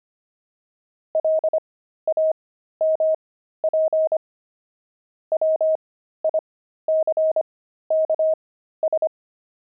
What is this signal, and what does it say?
Morse code, heard as a keyed tone, about 25 wpm, spaced as for 13 wpm, 630 Hz. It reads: LAMP WICKS
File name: pangram-fs-13-cs-25-nine